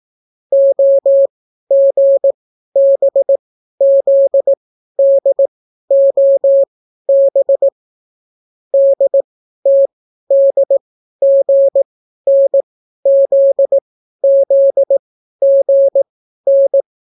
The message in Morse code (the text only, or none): OGBZDOB DTDGNZZGN